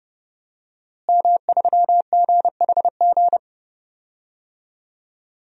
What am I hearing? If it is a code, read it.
Morse code, heard as a keyed tone, about 30 wpm, 700 Hz. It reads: M3GHG
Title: DXCC-100 Call Signs - 5 Characters - Repeated 2x - 30wpm